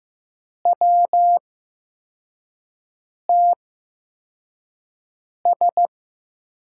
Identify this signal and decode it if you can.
Morse code, heard as a keyed tone, about 15 wpm, spaced as for 4 wpm, 700 Hz. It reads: WTS
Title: USA Call Signs - 5 Characters - 8x Character Spacing - 15wpm